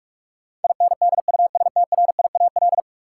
Morse code, heard as a keyed tone, about 46 wpm, 700 Hz. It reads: INDUSTRIAL
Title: Sets of 5 Words - New Word Drill - 45wpm